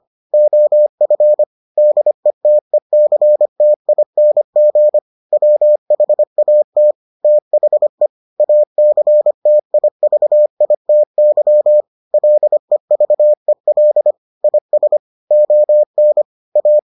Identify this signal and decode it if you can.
Morse code, heard as a keyed tone, about 25 wpm, 600 Hz. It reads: OF DETECTING WHAT THE ACTIVITY LEVEL IS ON A